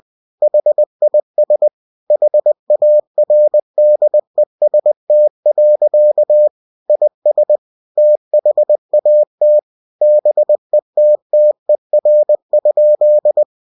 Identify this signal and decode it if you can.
Morse code, heard as a keyed tone, about 20 wpm, 600 Hz. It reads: HIS HARDEST. IS THAT BETTER?
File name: cw-books / winnie-the-pooh / winnie-20wpm-page-0088